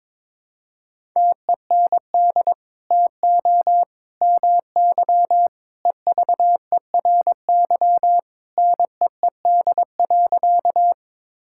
Morse code, heard as a keyed tone, about 22 wpm, 700 Hz. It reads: TEND TO MY EVERY NEED.
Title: Sentences from Top 1000 Words - Review Part 5 - 22wpm